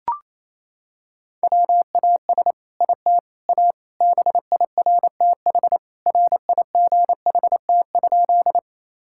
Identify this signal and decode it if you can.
Morse code, heard as a keyed tone, about 28 wpm, 700 Hz. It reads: WAS IT A BIRTH RIGHT?
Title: Sentences from Top 1000 Words - Review Part 5 - 28wpm